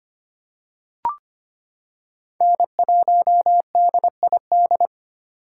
Morse code, heard as a keyed tone, about 25 wpm, 700 Hz. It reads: N1DID
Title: K1USN SST QSO Practice - 25wpm